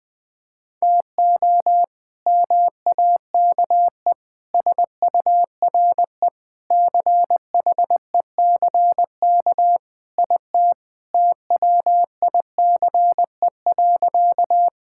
Morse code, heard as a keyed tone, about 20 wpm, 700 Hz. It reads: TO MAKE SURE CHECK IT TWICE.
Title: Sentences with 3-Word Phrases - Spoken Beginning - 20wpm